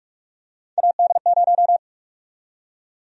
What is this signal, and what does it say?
Morse code, heard as a keyed tone, about 48 wpm, 700 Hz. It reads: AD0
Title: USA Call Sign Prefixes - Repeated 2x - 45wpm